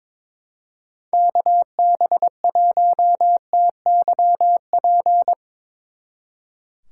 Morse code, heard as a keyed tone, about 22 wpm, 700 Hz. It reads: KB1TYP
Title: Speed Racing - USA 2x3 Call Signs 22wpm